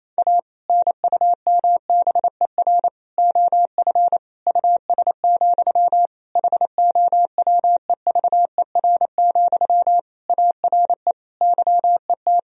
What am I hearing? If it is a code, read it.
Morse code, heard as a keyed tone, about 28 wpm, 700 Hz. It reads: A NUMBER OF US, HOWEVER, ARE YET
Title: Sentences from Top 1000 Words - Review Part 1 - 28wpm